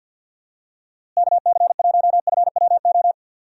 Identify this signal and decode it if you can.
Morse code, heard as a keyed tone, about 46 wpm, 700 Hz. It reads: KC1FWO
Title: USA Call Signs - 6 Characters - Repeated 3x - 50wpm